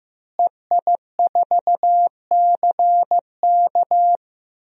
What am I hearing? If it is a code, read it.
Morse code, heard as a keyed tone, about 15 wpm, 700 Hz. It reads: EI4CK
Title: DXCC-100 Call Signs - 5 Characters - Repeated 3x - 15wpm